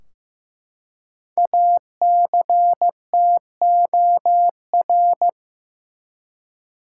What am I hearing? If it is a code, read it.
Morse code, heard as a keyed tone, about 15 wpm, 700 Hz. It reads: ACTOR
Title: Five Letter Words - 15wpm